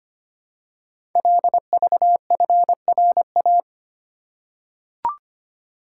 Morse code, heard as a keyed tone, about 25 wpm, 700 Hz. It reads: LVFRA
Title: Code Groups - 5 Letters - 25wpm